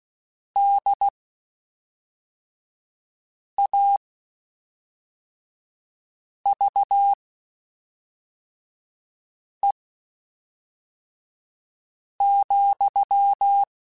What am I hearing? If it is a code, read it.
Morse code, heard as a keyed tone, about 16 wpm, spaced as for 3 wpm, 800 Hz. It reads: DAVE,
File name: (spaced) / QSOs Spaced-mp3s / kc7sac @5 sp